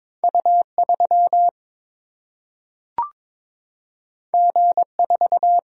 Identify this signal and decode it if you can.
Morse code, heard as a keyed tone, about 22 wpm, 700 Hz. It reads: U 3 G 4
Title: Single-Letter Single-Number 22wpm